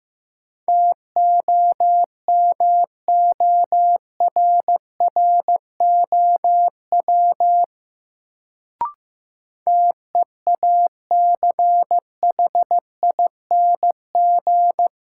Morse code, heard as a keyed tone, about 15 wpm, 700 Hz. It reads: TOMORROW TEACHING
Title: Sentences from Top 1100 Words - New Word Drill - 15wpm